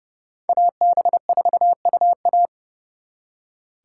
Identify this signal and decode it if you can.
Morse code, heard as a keyed tone, about 38 wpm, 700 Hz. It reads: AB4UA